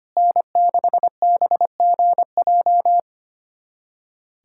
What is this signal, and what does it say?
Morse code, heard as a keyed tone, about 25 wpm, 700 Hz. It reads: N6BGJ
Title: USA Callsign DE Callsign - 5 Characters - 25wpm